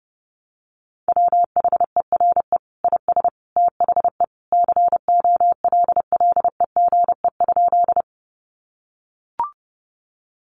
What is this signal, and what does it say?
Morse code, heard as a keyed tone, about 30 wpm, 700 Hz. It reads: WHERE IS THE COLLEGE?
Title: Sentences from Top 1300 Words 30wpm